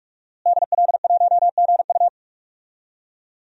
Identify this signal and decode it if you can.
Morse code, heard as a keyed tone, about 46 wpm, 700 Hz. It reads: DL1GU
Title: DXCC-100 Call Signs - 5 Characters - Repeated 2x - 45wpm